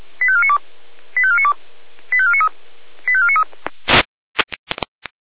Five-tone selective call (ZVEI), heard as quick runs of five tones, tones of about 70 ms. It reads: 85492 85492 85493 85493
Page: ZVEI